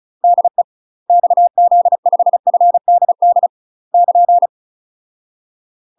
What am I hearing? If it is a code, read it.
Morse code, heard as a keyed tone, about 36 wpm, 700 Hz. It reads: DE XZ5FDD <KN>
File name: UK MASTER QSOs TRAINING / 35WPM / UK MASTER QSOs 0029